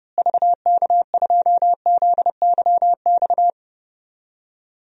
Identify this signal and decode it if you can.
Morse code, heard as a keyed tone, about 30 wpm, 700 Hz. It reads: VK2ZYX
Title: DXCC-100 Call Signs - 6 Characters - Repeated 3x - 30wpm